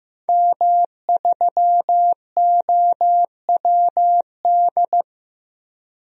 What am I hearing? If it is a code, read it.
Morse code, heard as a keyed tone, about 15 wpm, 700 Hz. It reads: M3OWD
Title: DXCC-100 Call Signs - 5 Characters - 15wpm